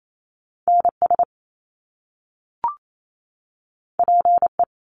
Morse code, heard as a keyed tone, about 28 wpm, 700 Hz. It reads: NS PE
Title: Canadian Provinces and Territories 28wpm